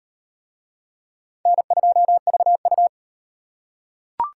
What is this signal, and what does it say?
Morse code, heard as a keyed tone, about 34 wpm, 700 Hz. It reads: N2VU